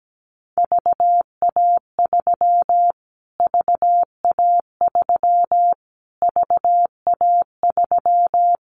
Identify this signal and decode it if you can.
Morse code, heard as a keyed tone, about 17 wpm, 700 Hz. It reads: VA3 VA3 VA3
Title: Callsign Prefixes for Top 25 DXCC - Repeated 3x - 17wpm